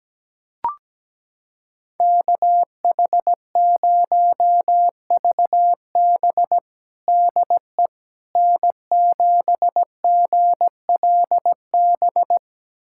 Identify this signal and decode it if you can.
Morse code, heard as a keyed tone, about 17 wpm, 700 Hz. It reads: KH0VB DE N7GLB
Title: USA Callsign DE Callsign - 5 Characters - 17wpm